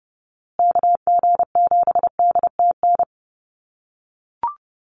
Morse code, heard as a keyed tone, about 28 wpm, 700 Hz. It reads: KG7DTN
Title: USA Call Signs Repeated 2x 30wpm